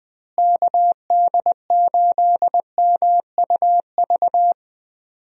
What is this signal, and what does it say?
Morse code, heard as a keyed tone, about 20 wpm, 700 Hz. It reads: KD8MUV
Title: USA Call Signs - 6 Characters - Repeated 3x - 20wpm